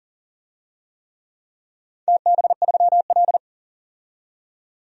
Morse code, heard as a keyed tone, about 37 wpm, 700 Hz. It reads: TB3L
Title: DXCC-100 Call Signs - 4 Characters - 40wpm